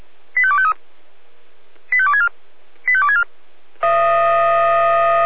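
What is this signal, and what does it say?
Five-tone selective call (ZVEI), heard as quick runs of five tones, tones of about 70 ms. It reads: 85253 85264 85264
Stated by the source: ZVEI five-tone call 85220 received by SDR, from audio.f85360.feuerwehr-orlamuende.org fire brigade server.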